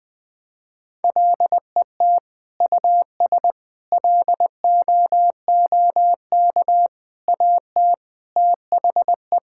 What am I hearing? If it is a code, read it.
Morse code, heard as a keyed tone, about 20 wpm, 700 Hz. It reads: LET US LOOK AT THE
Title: Sentences from Top 1000 Words - Review Part 5 - 20wpm